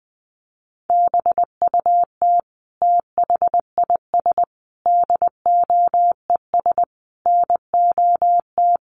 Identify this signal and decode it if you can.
Morse code, heard as a keyed tone, about 20 wpm, 700 Hz. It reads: BUT THIS DOES NOT
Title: Sentences from Top 1400 Words 20wpm